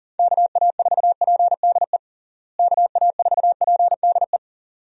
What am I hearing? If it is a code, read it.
Morse code, heard as a keyed tone, about 40 wpm, 700 Hz. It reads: KA4PDE KA4PDE
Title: USA Call Signs - 6 Characters - Repeated 2x - 40wpm